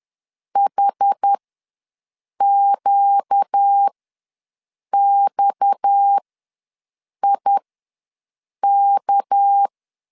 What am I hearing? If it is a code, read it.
Morse code, heard as a keyed tone, about 10 wpm, spaced as for 6 wpm, 800 Hz. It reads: HQXIK
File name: LEZIONI / 6 PKX0 / 6 PKX ws 10wpm 3706